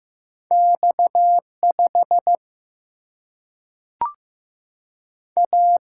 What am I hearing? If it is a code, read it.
Morse code, heard as a keyed tone, about 15 wpm, 700 Hz. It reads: X 5 A
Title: Single-Letter Single-Number 15wpm